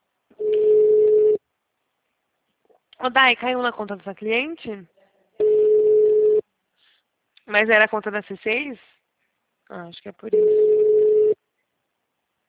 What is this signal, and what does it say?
Ringback tone (Europe, 425 Hz), with speech.